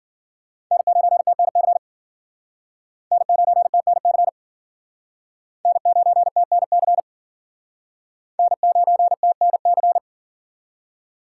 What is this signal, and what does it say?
Morse code, heard as a keyed tone, about 53 wpm, 700 Hz. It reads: N9TNC N9TNC N9TNC N9TNC